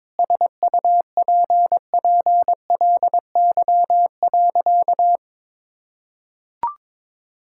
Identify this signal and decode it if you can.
Morse code, heard as a keyed tone, about 22 wpm, 700 Hz. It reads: SUPPLY.
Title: Sentences from Top 1000 Words - Review Part 5 - 22wpm